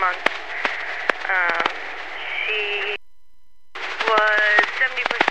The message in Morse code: none

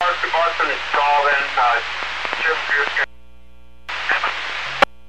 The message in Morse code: none

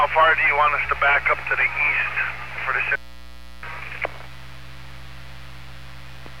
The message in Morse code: none

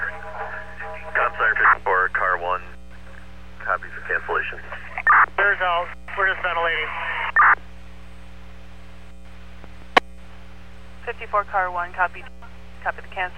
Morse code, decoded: none

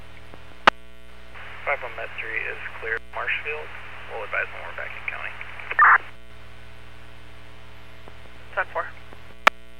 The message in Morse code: none

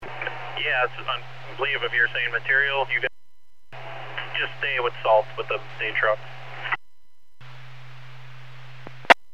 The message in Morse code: none

